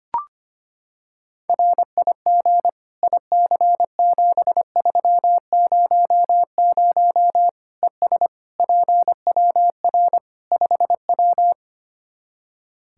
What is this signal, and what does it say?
Morse code, heard as a keyed tone, about 25 wpm, 700 Hz. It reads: RIG IC7300 ES PWR 5W
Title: K1USN SST QSO Practice 25wpm